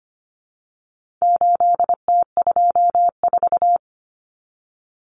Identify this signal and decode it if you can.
Morse code, heard as a keyed tone, about 25 wpm, 700 Hz. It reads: 8T24
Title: Common Cut Numbers - 4-Digit 25wpm